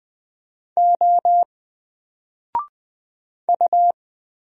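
Morse code, heard as a keyed tone, about 20 wpm, 700 Hz. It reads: O U
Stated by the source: Single letter and number Morse drill at 20 wpm, from N4LCD.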